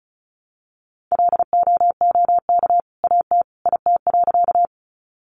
Morse code, heard as a keyed tone, about 34 wpm, 700 Hz. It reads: LOOK AT IT.
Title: Sentences from Top 100 Words 35wpm